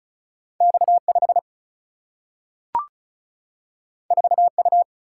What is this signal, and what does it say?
Morse code, heard as a keyed tone, about 36 wpm, 700 Hz. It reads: X5 4U
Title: Single-Letter Single-Number 35wpm